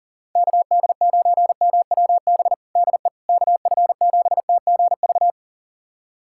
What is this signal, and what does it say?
Morse code, heard as a keyed tone, about 40 wpm, 700 Hz. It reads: KD9MWB DE KF7TGV
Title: USA Callsign DE Callsign - 6 Characters - 40wpm